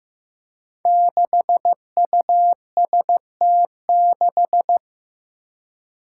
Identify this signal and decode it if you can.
Morse code, heard as a keyed tone, about 16 wpm, 700 Hz. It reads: 6UST6
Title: Five Challenging Letter-Number Combinations 15wpm